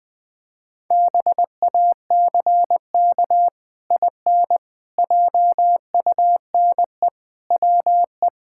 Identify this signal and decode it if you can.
Morse code, heard as a keyed tone, about 21 wpm, 700 Hz. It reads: BACK IN JUNE WE
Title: Pangram Challenge v2 - 20wpm